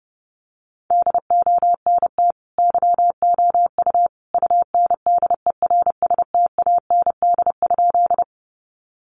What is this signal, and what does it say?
Morse code, heard as a keyed tone, about 30 wpm, 700 Hz. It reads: DONT YOU UNDERSTAND?
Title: Sentences from Top 600 Words 30wpm